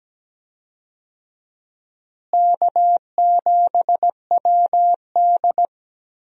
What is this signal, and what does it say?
Morse code, heard as a keyed tone, about 17 wpm, 700 Hz. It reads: K7WD
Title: USA Call Signs - 4 Characters - Repeated 3x - 17wpm